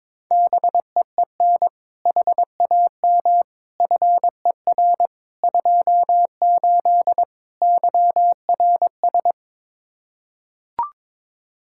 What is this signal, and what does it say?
Morse code, heard as a keyed tone, about 22 wpm, 700 Hz. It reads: BEEN HAM FER 28 YRS
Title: Practice QSO - Exchange 3 - More Info - 22wpm